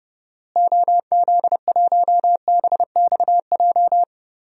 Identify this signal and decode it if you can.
Morse code, heard as a keyed tone, about 30 wpm, 700 Hz. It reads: OZ1BXJ
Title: DXCC-100 Call Signs - 6 Characters - Repeated 3x - 30wpm